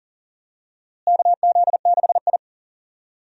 Morse code, heard as a keyed tone, about 40 wpm, 700 Hz. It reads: KZ6I
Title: USA Call Signs - 4 Characters - Repeated 3x - 40wpm